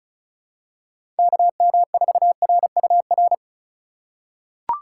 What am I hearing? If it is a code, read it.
Morse code, heard as a keyed tone, about 36 wpm, 690 Hz. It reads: KM4RUR
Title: USA Call Signs - 6 Characters - 6x Character Spacing - 35wpm